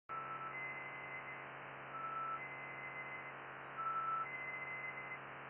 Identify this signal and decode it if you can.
Emergency Alert System (SAME) tones (header bursts), heard as harsh buzzing bursts. It is heard from 0.5 to 5.2 s.